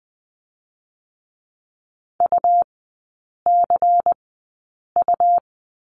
Morse code, heard as a keyed tone, about 20 wpm, 700 Hz. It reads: U C U
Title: 100 Lesson 17e - Letter u - Three Characters 20wpm